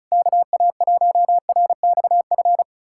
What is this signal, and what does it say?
Morse code, heard as a keyed tone, about 37 wpm, 700 Hz. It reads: KA1RXF
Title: SOTA QSO Practice 35wpm